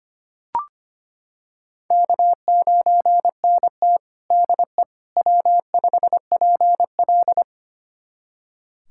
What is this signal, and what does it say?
Morse code, heard as a keyed tone, about 25 wpm, 700 Hz. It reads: K9NT DE W5PL
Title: USA Callsign DE Callsign - 4 Characters - 25wpm